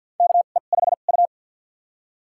Morse code, heard as a keyed tone, about 49 wpm, spaced as for 36 wpm, 700 Hz. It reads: KE5U
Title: USA Call Signs - 4 Characters - 2x Character Spacing - 50wpm